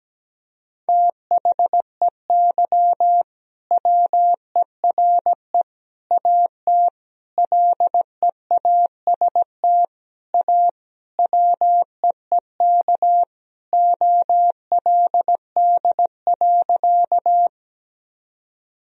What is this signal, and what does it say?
Morse code, heard as a keyed tone, about 17 wpm, 700 Hz. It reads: THEY WERE AT LEAST A WEEK OLD.
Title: Sentences from Top 1000 Words - Review Part 2 - 17wpm